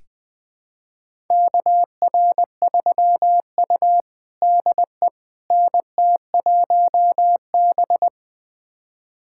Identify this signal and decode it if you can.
Morse code, heard as a keyed tone, about 20 wpm, 700 Hz. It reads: KR3U DE NT1B
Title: USA Callsign DE Callsign - 4 Characters - 20wpm